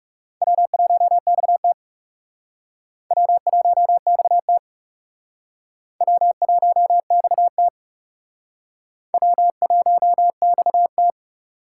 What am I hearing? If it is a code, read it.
Morse code, heard as a keyed tone, about 36 wpm, 700 Hz. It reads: W1XT W1XT W1XT W1XT
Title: Speed Racing - USA 1x2 Call Signs 30wpm